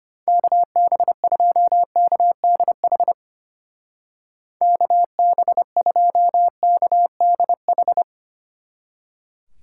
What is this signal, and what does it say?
Morse code, heard as a keyed tone, about 32 wpm, 700 Hz. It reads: KB2KDH KB2KDH